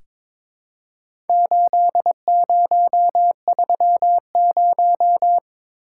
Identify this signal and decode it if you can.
Morse code, heard as a keyed tone, about 22 wpm, 690 Hz. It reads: 8030
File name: practice / 22wpm / Four Letter-Number Combinations - 22wpm